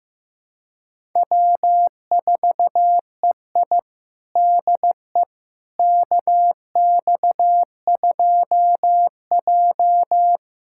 Morse code, heard as a keyed tone, about 16 wpm, 700 Hz. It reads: W4EI DE KX2J
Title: USA Callsign DE Callsign - 4 Characters - 15wpm